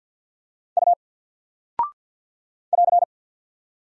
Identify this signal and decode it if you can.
Morse code, heard as a keyed tone, about 49 wpm, 700 Hz. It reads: U <AR>
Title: Single Character (Letter:Number:Punctuation:Prosign) 50wpm